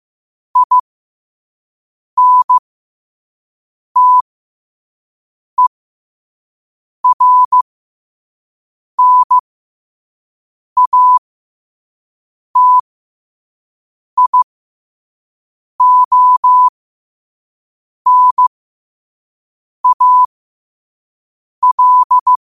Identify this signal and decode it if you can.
Morse code, heard as a keyed tone, about 15 wpm, spaced as for 5 wpm, 1000 Hz. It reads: INTERNATIONAL
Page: New quotes every day in morse code at 5 Words per minute.